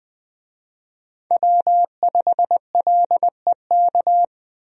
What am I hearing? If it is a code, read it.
Morse code, heard as a keyed tone, about 20 wpm, 700 Hz. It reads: W5LEK
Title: USA Call Signs - 5 Characters - Repeated 3x - 20wpm